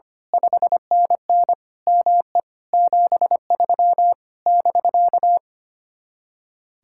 Morse code, heard as a keyed tone, about 24 wpm, 700 Hz. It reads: 5NN ME 73 <BK>
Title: Speed-Racing Common QSO Phrases - 25wpm